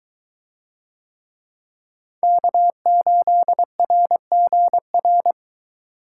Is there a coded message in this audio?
Yes: Morse code from 2.2 to 5.3 s, heard as a keyed tone, about 21 wpm, 700 Hz. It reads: K8RGR